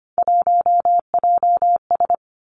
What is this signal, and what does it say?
Morse code, heard as a keyed tone, about 25 wpm, 700 Hz. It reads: 1JS
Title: Three Challenging Letter-Number Combinations 25wpm